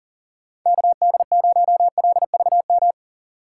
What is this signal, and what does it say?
Morse code, heard as a keyed tone, about 40 wpm, 700 Hz. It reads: KD0LVM
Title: USA Call Signs - 6 Characters - 3x Character Spacing - 40wpm